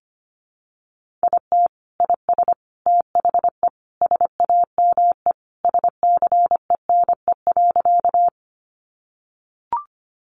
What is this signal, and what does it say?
Morse code, heard as a keyed tone, about 25 wpm, 700 Hz. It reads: IT IS THE SAME SCENE.
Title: Sentences from Top 1100 Words 25wpm